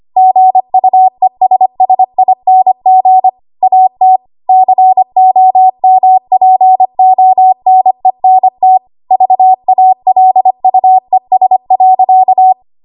Morse code, heard as a keyed tone, about 25 wpm, 750 Hz. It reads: GUESSING AT COMPONENT VALUES.
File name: Morse / 25WPM